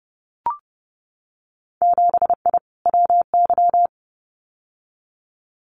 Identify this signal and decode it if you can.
Morse code, heard as a keyed tone, about 30 wpm, 700 Hz. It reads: 7I WY
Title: Winter Field Day Exchanges Encore 30wpm